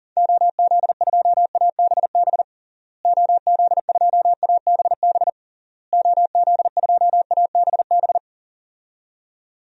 Morse code, heard as a keyed tone, about 40 wpm, 700 Hz. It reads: OZ2ABB OZ2ABB OZ2ABB
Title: DXCC-100 Call Signs - 6 Characters - Repeated 3x - 40wpm